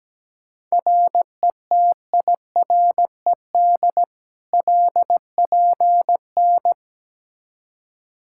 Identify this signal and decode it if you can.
Morse code, heard as a keyed tone, about 17 wpm, 700 Hz. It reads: RETIRED LPN
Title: Practice QSO - Exchange 3 - More Info - 17wpm